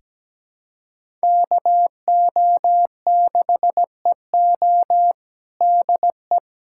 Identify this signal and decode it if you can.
Morse code, heard as a keyed tone, about 17 wpm, 700 Hz. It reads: KO6EO DE
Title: USA Callsign DE Callsign - 5 Characters - 17wpm